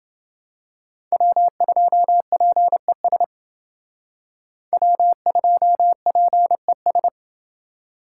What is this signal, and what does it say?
Morse code, heard as a keyed tone, about 28 wpm, 700 Hz. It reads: W2PES W2PES